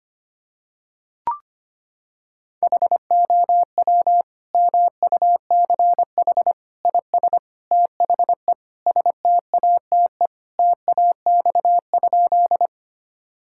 Morse code, heard as a keyed tone, about 25 wpm, 700 Hz. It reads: HOW MUCH IS THE STATE TAX?
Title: Sentences from Top 1000 Words - Review Part 5 - 25wpm